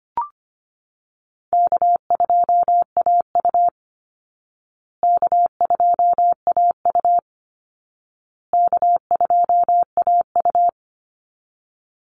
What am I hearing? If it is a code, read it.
Morse code, heard as a keyed tone, about 25 wpm, 700 Hz. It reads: K2AU K2AU K2AU
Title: Easy Call Signs Repeated 3x - 25wpm